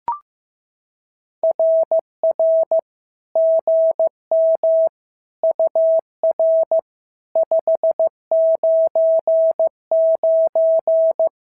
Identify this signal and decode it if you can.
Morse code, heard as a keyed tone, about 15 wpm, 650 Hz. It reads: RR GM UR 599
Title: SOTA QSO Practice 15wpm